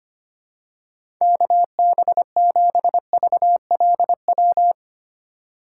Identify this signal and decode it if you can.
Morse code, heard as a keyed tone, about 25 wpm, 700 Hz. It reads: KB7VLW
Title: USA Call Signs - 6 Characters - 2x Character Spacing - 25wpm